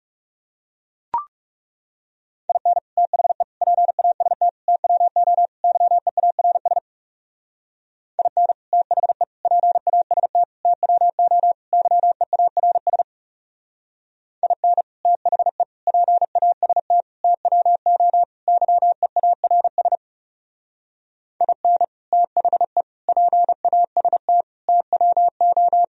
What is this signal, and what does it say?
Morse code, heard as a keyed tone, about 34 wpm, 700 Hz. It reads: IN THE PAST TWO YEARS IN THE PAST TWO YEARS IN THE PAST TWO YEARS IN THE PAST TWO
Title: Speed-Racing - Sets of 5 Words 30wpm